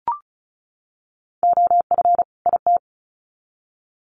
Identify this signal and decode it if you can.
Morse code, heard as a keyed tone, about 34 wpm, 700 Hz. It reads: OF IT
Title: Sets of 2 Words 35wpm